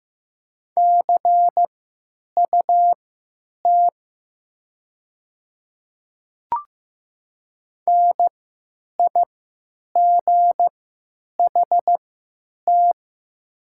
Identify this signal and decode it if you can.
Morse code, heard as a keyed tone, about 15 wpm, spaced as for 9 wpm, 700 Hz. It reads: CUT NIGHT
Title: 3x Character Spacing - Top 500 Words - 15wpm